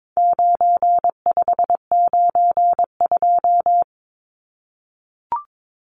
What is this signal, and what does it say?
Morse code, heard as a keyed tone, about 22 wpm, 700 Hz. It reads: 9592
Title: Numbers - 4-Digit 22wpm